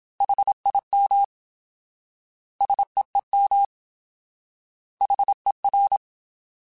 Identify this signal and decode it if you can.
Morse code, heard as a keyed tone, about 27 wpm, 800 Hz. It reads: HIM SEEM HER
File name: Alphabetical / Alphabetical mp3s / S vs H @25